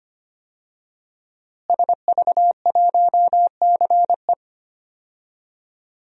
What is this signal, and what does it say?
Morse code, heard as a keyed tone, about 25 wpm, 700 Hz. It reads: SV1CE
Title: DXCC-100 Call Signs - 5 Characters - Repeated 2x - 25wpm